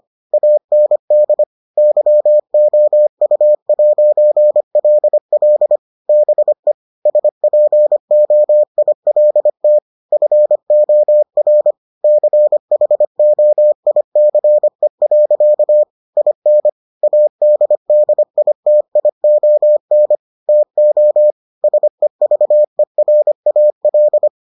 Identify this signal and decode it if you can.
Morse code, heard as a keyed tone, about 25 wpm, 600 Hz. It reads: AND YOU'LL BE SPOILT FOR CHOICE. IN ADDITION TO SEVERAL